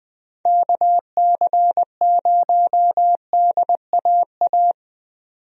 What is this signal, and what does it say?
Morse code, heard as a keyed tone, about 20 wpm, 700 Hz. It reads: KC0DAA
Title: USA Call Signs - 6 Characters - Repeated 3x - 20wpm